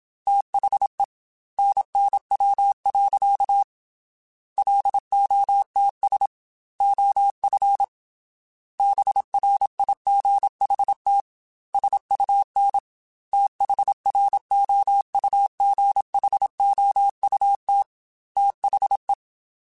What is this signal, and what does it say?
Morse code, heard as a keyed tone, about 27 wpm, 800 Hz. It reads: THE NNW. LOTS OF BRIGHT SUN THROUGHOUT THE
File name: Weather Sets / Weather Sets-mp3s / Weather 5 12 @25